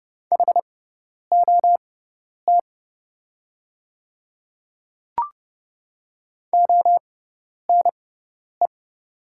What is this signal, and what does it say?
Morse code, heard as a keyed tone, about 29 wpm, spaced as for 10 wpm, 700 Hz. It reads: HOT ONE
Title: 6x Character Spacing - Top 100 Words - 30wpm